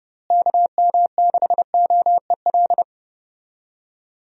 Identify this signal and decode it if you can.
Morse code, heard as a keyed tone, about 30 wpm, 700 Hz. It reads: KM6OEL
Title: USA Call Signs - 6 Characters - Repeated 3x - 30wpm